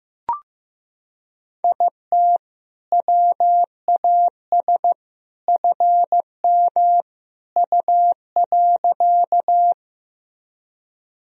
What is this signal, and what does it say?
Morse code, heard as a keyed tone, about 15 wpm, 700 Hz. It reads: IT WAS FM U.
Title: Sentences from Top 100 Words and Abbreviations 15wpm